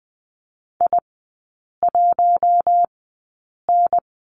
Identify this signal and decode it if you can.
Morse code, heard as a keyed tone, about 20 wpm, spaced as for 8 wpm, 700 Hz. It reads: I1N
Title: 041 Lesson 8e - Number 1 - Three Characters 20wpm